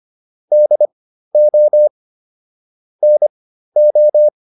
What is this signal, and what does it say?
Morse code, heard as a keyed tone, about 25 wpm, spaced as for 13 wpm, 600 Hz. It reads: DO NO